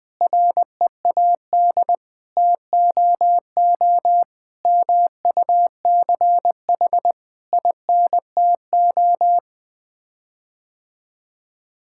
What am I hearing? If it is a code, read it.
Morse code, heard as a keyed tone, about 20 wpm, 700 Hz. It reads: READ TOO MUCH INTO
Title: Sets of 9 Words - 20wpm